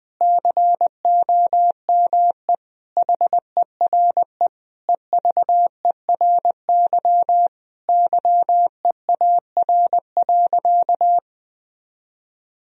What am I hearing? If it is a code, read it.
Morse code, heard as a keyed tone, about 20 wpm, 700 Hz. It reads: COME HERE EVERY YEAR.
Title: Sentences from Top 1000 Words - Review Part 1 - 20wpm